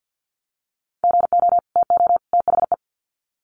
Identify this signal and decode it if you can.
Morse code, heard as a keyed tone, about 50 wpm, 700 Hz. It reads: GO TO THE
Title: Sets of 3 Words 50wpm